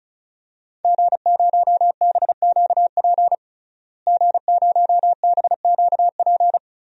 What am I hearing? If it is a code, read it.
Morse code, heard as a keyed tone, about 35 wpm, 700 Hz. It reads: G0BQP G0BQP
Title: DXCC-100 Call Signs - 5 Characters - Repeated 3x - 35wpm